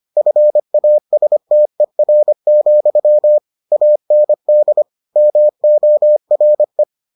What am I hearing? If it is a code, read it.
Morse code, heard as a keyed tone, about 25 wpm, 600 Hz. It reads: FASTER, AND MORE